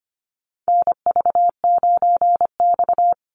Morse code, heard as a keyed tone, about 25 wpm, 700 Hz. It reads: NV9X
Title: Call Signs Repeated 3x - 25wpm